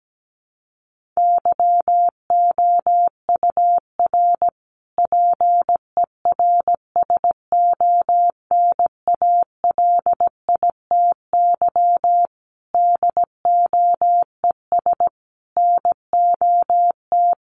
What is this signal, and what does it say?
Morse code, heard as a keyed tone, about 17 wpm, 700 Hz. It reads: YOUR PERSONALITY DOES NOT
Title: Sentences from Top 1400 Words 17wpm